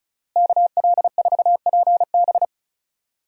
Morse code, heard as a keyed tone, about 36 wpm, 700 Hz. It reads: KL4PB
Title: USA Call Signs - 5 Characters - Repeated 3x - 35wpm